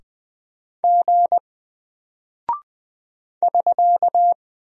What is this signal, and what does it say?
Morse code, heard as a keyed tone, about 20 wpm, 700 Hz. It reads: G <SK>
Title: Single Character (Letter:Number:Punctuation:Prosign) 20wpm